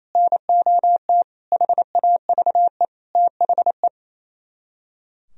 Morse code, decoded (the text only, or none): NOT HAVE THE